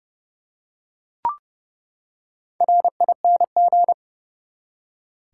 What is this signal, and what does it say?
Morse code, heard as a keyed tone, about 30 wpm, 700 Hz. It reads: RING
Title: Four Letter Words - 30wpm